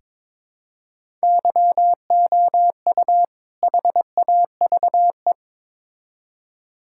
Morse code, heard as a keyed tone, about 22 wpm, 700 Hz. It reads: YOU HAVE
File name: Build A Phrase - 2-Word Phrases - 22wpm